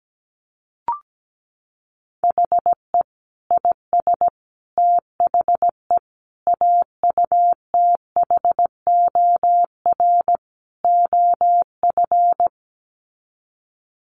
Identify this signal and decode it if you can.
Morse code, heard as a keyed tone, about 17 wpm, 700 Hz. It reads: HE IS THE AUTHOR OF
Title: Sets of 5 Words Encore2 17wpm